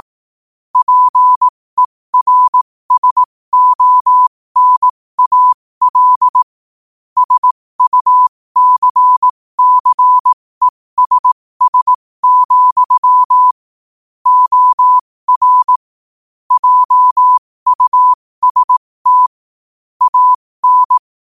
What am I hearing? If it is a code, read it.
Morse code, heard as a keyed tone, about 18 wpm, spaced as for 15 wpm, 1000 Hz. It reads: PERSONAL SUCCESS, OR JUST AN
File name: qotdfiles / 15wpm